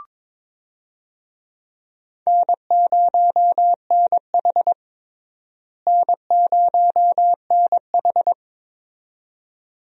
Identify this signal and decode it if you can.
Morse code, heard as a keyed tone, about 22 wpm, 700 Hz. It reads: N0NH N0NH
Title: USA Call Signs - 4 Characters - Repeated 2x - 22wpm